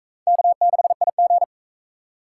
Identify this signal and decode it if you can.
Morse code, heard as a keyed tone, about 37 wpm, 700 Hz. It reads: K6IG